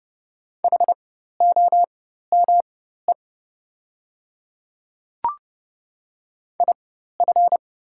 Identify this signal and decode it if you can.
Morse code, heard as a keyed tone, about 29 wpm, spaced as for 14 wpm, 700 Hz. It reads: HOME IF
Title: 4x Character Spacing - Top 200 Words - 30wpm